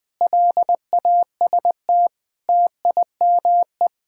Morse code, heard as a keyed tone, about 20 wpm, 700 Hz. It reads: LAST TIME
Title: Sentences with 7-Word Phrases - Spoken Beginning - 20wpm